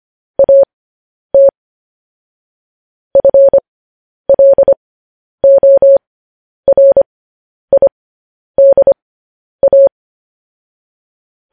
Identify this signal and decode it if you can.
Morse code, heard as a keyed tone, about 25 wpm, spaced as for 10 wpm, 550 Hz. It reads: AT FLORIDA
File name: podcast / News headlines-1x-25x10